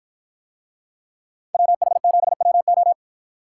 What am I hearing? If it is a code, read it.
Morse code, heard as a keyed tone, about 53 wpm, 700 Hz. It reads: WH7WO